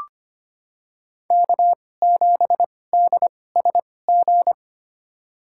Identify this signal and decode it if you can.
Morse code, heard as a keyed tone, about 25 wpm, spaced as for 18 wpm, 700 Hz. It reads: K7DSG
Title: USA Call Signs - 5 Characters - 2x Character Spacing - 25wpm